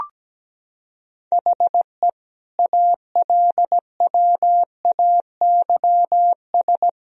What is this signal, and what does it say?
Morse code, heard as a keyed tone, about 17 wpm, 700 Hz. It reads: HE ALWAYS
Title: Sentences from Top 1000 Words - Review Part 4 - 17wpm